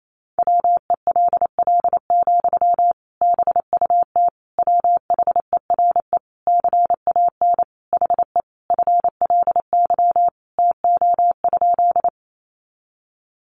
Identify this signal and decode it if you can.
Morse code, heard as a keyed tone, about 28 wpm, 700 Hz. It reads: WELL, BUT WHERE CAN HE FLY TO?
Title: Sentences from Top 500 Words 28wpm